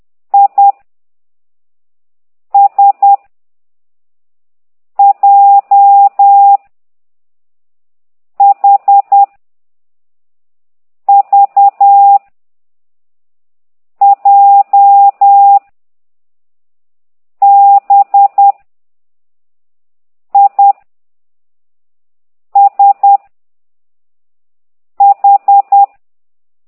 Morse code, heard as a keyed tone, about 10 wpm, 800 Hz. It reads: I S J H V J B I S H
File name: ct2idw / exerc9